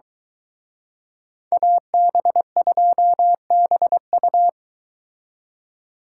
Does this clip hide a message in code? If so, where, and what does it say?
Yes: Morse code from 1.5 to 4.5 s, heard as a keyed tone, about 23 wpm, 700 Hz. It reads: AB2BU